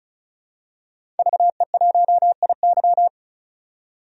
Morse code, heard as a keyed tone, about 36 wpm, 700 Hz. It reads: VE1IY
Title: DXCC-100 Call Signs - 5 Characters - 35wpm